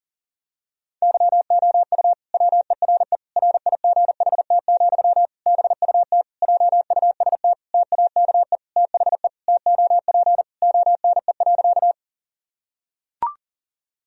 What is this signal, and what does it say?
Morse code, heard as a keyed tone, about 40 wpm, 700 Hz. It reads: YOU WERE RIGHT, BUT JUST TAKE THE TOP ONE.
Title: Sentences from Top 1000 Words - Review Part 2 - 40wpm